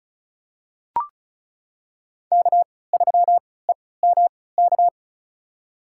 Morse code, heard as a keyed tone, about 34 wpm, spaced as for 20 wpm, 700 Hz. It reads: K3EMK
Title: USA Call Signs - 5 Characters - 3x Character Spacing - 35wpm